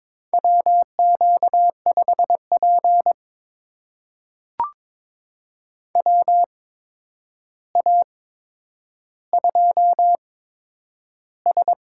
Morse code, heard as a keyed tone, about 22 wpm, spaced as for 9 wpm, 700 Hz. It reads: WQ5P WA2S
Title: USA Call Signs - 4 Characters - 8x Character Spacing - 22wpm